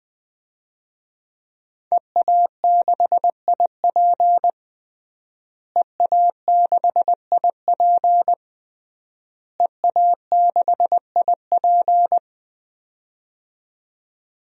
Morse code, heard as a keyed tone, about 20 wpm, 700 Hz. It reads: EA6IP EA6IP EA6IP
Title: DXCC-100 Call Signs - 5 Characters - Repeated 3x - 20wpm